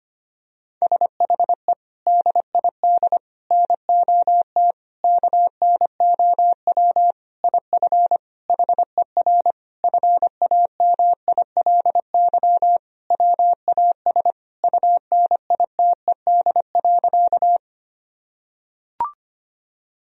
Morse code, heard as a keyed tone, about 25 wpm, 700 Hz. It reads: SHE DID NOT KNOW IF HER FAMILY WAS UNITED.
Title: Sentences from Top 1000 Words - Review Part 4 - 25wpm